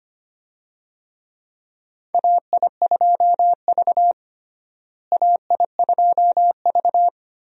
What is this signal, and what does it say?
Morse code, heard as a keyed tone, about 25 wpm, 700 Hz. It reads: AI2V AI2V
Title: USA Call Signs - 4 Characters - Repeated 3x - 25wpm